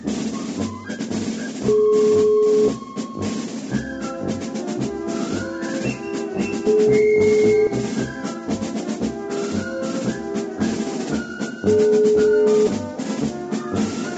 Ringback tone (Europe, 425 Hz), with noise.